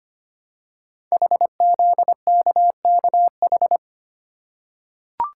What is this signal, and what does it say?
Morse code, heard as a keyed tone, about 25 wpm, 700 Hz. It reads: HZKKH
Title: Code Groups - 5 Letters - 25wpm